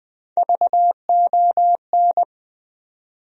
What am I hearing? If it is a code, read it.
Morse code, heard as a keyed tone, about 20 wpm, 700 Hz. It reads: VON
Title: USA Call Sign Postfixes - 3 Letters 20wpm